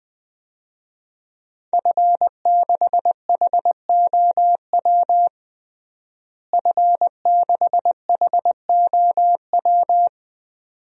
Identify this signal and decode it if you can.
Morse code, heard as a keyed tone, about 21 wpm, 700 Hz. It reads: F6HOW F6HOW
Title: DXCC-100 Call Signs - 5 Characters - Repeated 2x - 20wpm